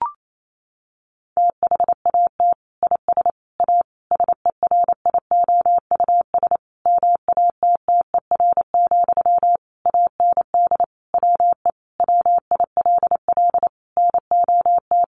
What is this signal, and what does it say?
Morse code, heard as a keyed tone, about 28 wpm, 700 Hz. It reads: THAT IS A SERIOUS MATTER, AND WE WILL NOT
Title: Sentences from Top 1100 Words 28wpm